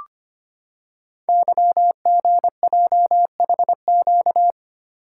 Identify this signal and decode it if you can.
Morse code, heard as a keyed tone, about 25 wpm, 700 Hz. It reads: YGJHQ
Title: Code Groups - 5 Letters - 25wpm